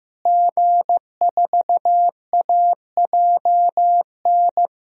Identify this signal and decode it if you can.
Morse code, heard as a keyed tone, about 15 wpm, 700 Hz. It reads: G4AJN
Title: DXCC-100 Call Signs - 5 Characters - Repeated 2x - 15wpm